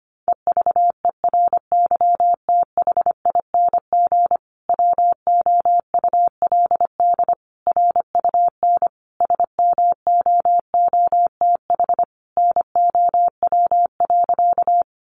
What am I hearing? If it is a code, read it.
Morse code, heard as a keyed tone, about 25 wpm, 700 Hz. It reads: EVERYTHING WOULD RUN SMOOTH NOW.
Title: Sentences from Top 1400 Words 25wpm